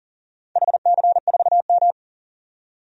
Morse code, heard as a keyed tone, about 38 wpm, 700 Hz. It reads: HC4M
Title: DXCC-100 Call Signs - 4 Characters - 40wpm